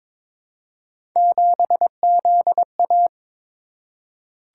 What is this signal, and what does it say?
Morse code, heard as a keyed tone, about 22 wpm, 700 Hz. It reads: 7ZA
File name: practice / 22wpm / Three Letter-Number Combinations - 22wpm